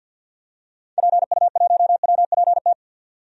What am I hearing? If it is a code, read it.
Morse code, heard as a keyed tone, about 52 wpm, 700 Hz. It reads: PU1WPT
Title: DXCC-100 Call Signs - 6 Characters - 50wpm